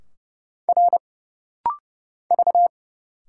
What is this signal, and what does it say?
Morse code, heard as a keyed tone, about 31 wpm, 700 Hz. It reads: R V
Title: Single Letters - Rapid-Fire 30wpm